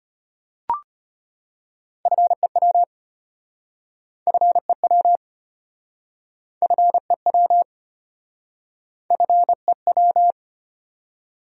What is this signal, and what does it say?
Morse code, heard as a keyed tone, about 33 wpm, 700 Hz. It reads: FEW FEW FEW FEW